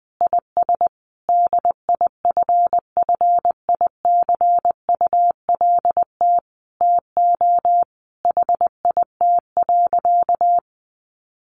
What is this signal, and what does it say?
Morse code, heard as a keyed tone, about 20 wpm, 700 Hz. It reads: IS DIFFICULT TO HIT.
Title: Sentences from Top 1200 Words 20wpm